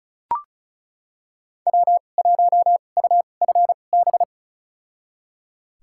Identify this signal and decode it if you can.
Morse code, heard as a keyed tone, about 36 wpm, spaced as for 25 wpm, 700 Hz. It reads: W1UFB
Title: USA Call Signs - 5 Characters - 2x Character Spacing - 35wpm